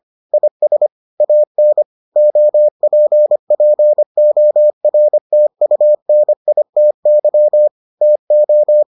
Morse code, heard as a keyed tone, about 25 wpm, 600 Hz. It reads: IS AN OPPORTUNITY TO